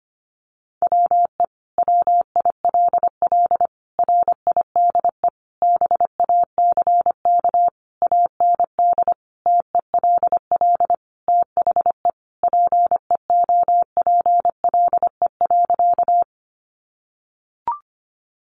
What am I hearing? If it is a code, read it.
Morse code, heard as a keyed tone, about 25 wpm, 700 Hz. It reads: WE WILL RIDE BACK AND TELL THE PEOPLE.
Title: Sentences from Top 1200 Words 25wpm